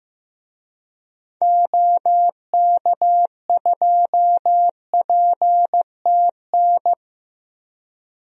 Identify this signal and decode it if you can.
Morse code, heard as a keyed tone, about 15 wpm, 700 Hz. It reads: OK2PTN
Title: DXCC-100 Call Signs - 6 Characters - 15wpm